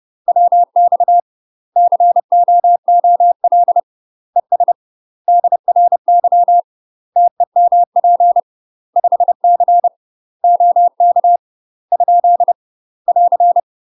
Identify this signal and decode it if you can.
Morse code, heard as a keyed tone, about 30 wpm, 700 Hz. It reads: WX COOL ES DRY TEMP 5C OK ? <AR>
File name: UK MASTER QSOs TRAINING / 30WPM / UK MASTER QSOs 0009